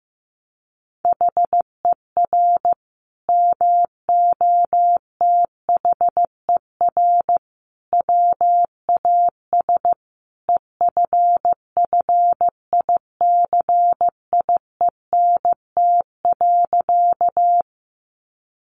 Morse code, heard as a keyed tone, about 15 wpm, 700 Hz. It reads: HER MOTHER WAS EFFICIENT.
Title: Sentences from Top 1300 Words 15wpm